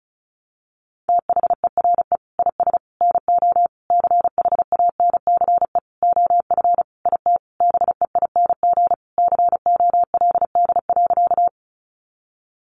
Morse code, heard as a keyed tone, about 34 wpm, 700 Hz. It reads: THERE IS NO CHANCE OF IT BEING COLD.
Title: Sentences from Top 800 Words 35wpm